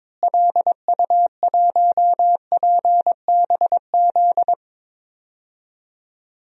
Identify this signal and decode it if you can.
Morse code, heard as a keyed tone, about 22 wpm, 700 Hz. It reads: LU1PBZ
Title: DXCC-100 Call Signs - 6 Characters - Repeated 2x - 22wpm